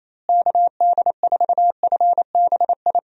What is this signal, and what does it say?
Morse code, heard as a keyed tone, about 27 wpm, 700 Hz. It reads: KD4FBI
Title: USA Call Signs - 6 Characters - Repeated 3x - 28wpm